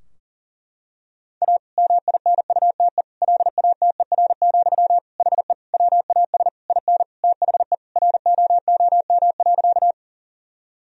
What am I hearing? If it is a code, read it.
Morse code, heard as a keyed tone, about 40 wpm, 700 Hz. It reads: A MINUTE LATER, HE WAS IN THE ROOM.
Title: Sentences from Top 1000 Words - Review Part 4 - 40wpm